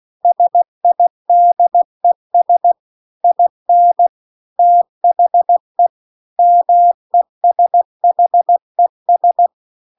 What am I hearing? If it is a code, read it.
Morse code, heard as a keyed tone, about 16 wpm, 700 Hz. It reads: SIDES IN THE MESHES